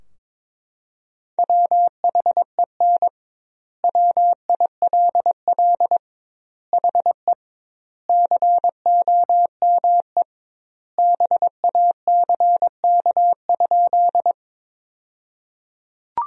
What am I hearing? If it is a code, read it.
Morse code, heard as a keyed tone, about 22 wpm, 700 Hz. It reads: WHEN WILL HE COME BACK?
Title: 2x Word-Spacing - Sentences from Top 200 Words - 22wpm